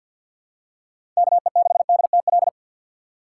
Morse code, heard as a keyed tone, about 52 wpm, 700 Hz. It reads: KE6DTL
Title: USA Call Signs - 6 Characters - 6x Character Spacing - 50wpm